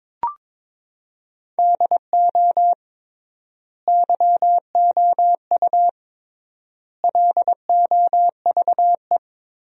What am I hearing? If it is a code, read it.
Morse code, heard as a keyed tone, about 23 wpm, 700 Hz. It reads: DO YOU LOVE
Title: 3x Word-Spacing - Sentences from Top 400 Words - 22wpm